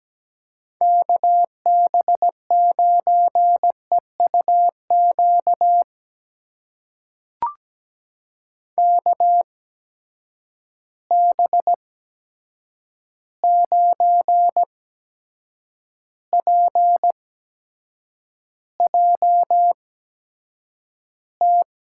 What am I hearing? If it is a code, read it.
Morse code, heard as a keyed tone, about 17 wpm, spaced as for 7 wpm, 700 Hz. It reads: KB9EUQ KB9PJT
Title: USA Call Signs - 6 Characters - 8x Character Spacing - 17wpm